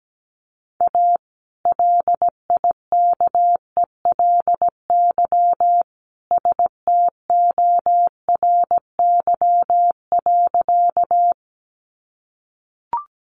Morse code, heard as a keyed tone, about 17 wpm, 700 Hz. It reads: A LIKELY STORY.
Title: Sentences from Top 600 Words 17wpm